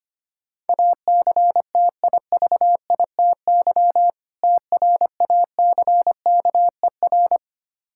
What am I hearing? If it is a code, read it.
Morse code, heard as a keyed tone, about 26 wpm, 700 Hz. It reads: ACTIVITY TRACKER
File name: practice / 25wpm / Difficult Words - Speed-Racing with Hints - 25wpm